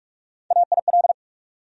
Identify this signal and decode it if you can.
Morse code, heard as a keyed tone, about 42 wpm, 700 Hz. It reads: AIL
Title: USA Call Sign Postfixes - 3 Letters 45wpm